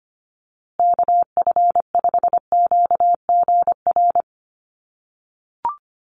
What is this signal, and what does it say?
Morse code, heard as a keyed tone, about 25 wpm, 700 Hz. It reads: KF5QGR
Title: USA Call Signs Repeated 3x 25wpm